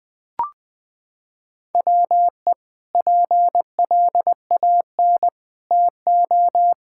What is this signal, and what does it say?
Morse code, heard as a keyed tone, about 20 wpm, 700 Hz. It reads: WE PLAN TO
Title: Sentences with 5-Word Phrases - 20wpm